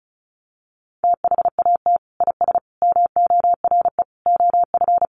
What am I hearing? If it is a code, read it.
Morse code, heard as a keyed tone, about 34 wpm, 700 Hz. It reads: THAT IS MORE OF
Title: Sentences from Top 1300 Words 35wpm